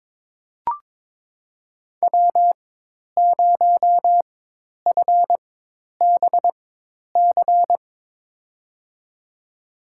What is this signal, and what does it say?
Morse code, heard as a keyed tone, about 23 wpm, spaced as for 10 wpm, 700 Hz. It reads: W0FBC
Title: USA Call Signs - 4x Character Spacing - 22wpm